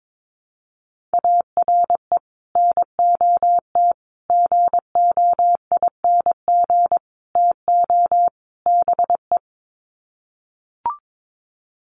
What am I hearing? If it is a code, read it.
Morse code, heard as a keyed tone, about 22 wpm, 700 Hz. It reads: ARE NOT GOING TO BE
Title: Sets of 5 Words Encore 22wpm